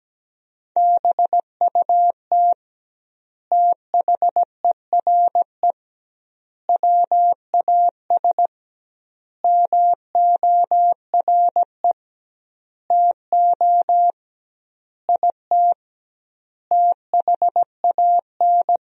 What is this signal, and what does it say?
Morse code, heard as a keyed tone, about 17 wpm, 700 Hz. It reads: BUT THERE WAS MORE TO IT THAN
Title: 2x Word-Spacing - Sentences from Top 100 Words - 17wpm